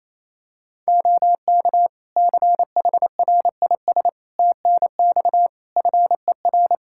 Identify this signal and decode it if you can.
Morse code, heard as a keyed tone, about 28 wpm, 700 Hz. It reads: OK CHRIS TNX FER
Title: Practice QSO - Exchange 4 - Ending - 28wpm